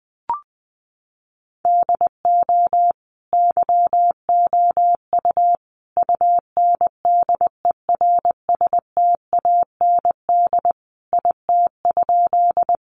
Morse code, heard as a keyed tone, about 20 wpm, 700 Hz. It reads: DO YOU UNDERSTAND IT?
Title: Sentences from Top 600 Words 20wpm